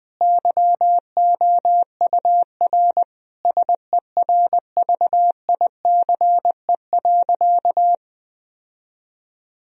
Morse code, heard as a keyed tone, about 20 wpm, 700 Hz. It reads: YOUR SERVICE.
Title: Sentences from Top 1000 Words - Review Part 4 - 20wpm